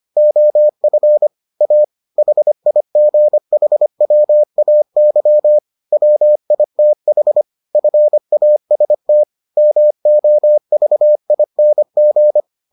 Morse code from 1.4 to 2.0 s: A

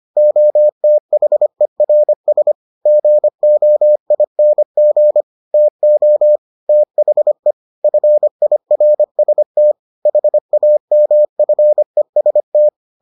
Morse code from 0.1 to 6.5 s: OTHERS GOING TO

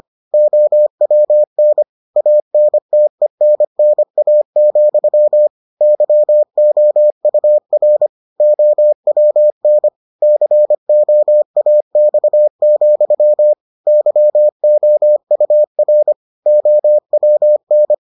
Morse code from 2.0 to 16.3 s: ANTENNA, YOUR OWN COAX, YOUR

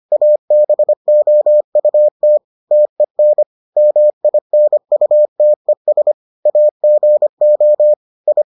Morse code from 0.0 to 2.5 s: ABOUT

